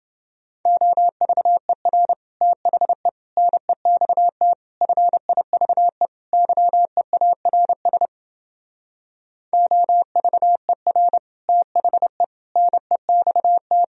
Morse code, heard as a keyed tone, about 30 wpm, 700 Hz. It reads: OVER THE NEXT FIVE YEARS OVER THE NEXT